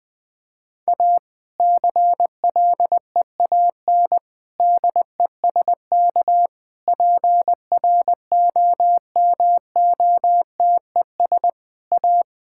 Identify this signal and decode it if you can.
Morse code, heard as a keyed tone, about 20 wpm, 700 Hz. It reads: A CLEAN DESK PROMOTES A
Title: Difficult Sentences with Hints - 20wpm